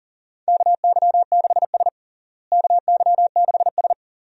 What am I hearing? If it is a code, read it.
Morse code, heard as a keyed tone, about 40 wpm, 700 Hz. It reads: KY6S KY6S
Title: USA Call Signs - 4 Characters - Repeated 2x - 40wpm